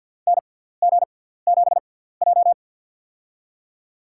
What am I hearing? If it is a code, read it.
Morse code, heard as a keyed tone, about 49 wpm, spaced as for 17 wpm, 700 Hz. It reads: NG7J
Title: USA Call Signs - 4 Characters - 6x Character Spacing - 50wpm